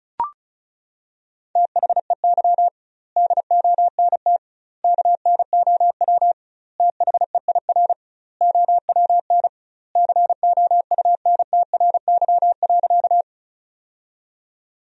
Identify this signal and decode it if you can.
Morse code, heard as a keyed tone, about 36 wpm, 700 Hz. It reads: THEY DONT KNOW THEIR OWN COUNTRY.
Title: 2x Word-Spacing - Sentences from Top 300 Words - 35wpm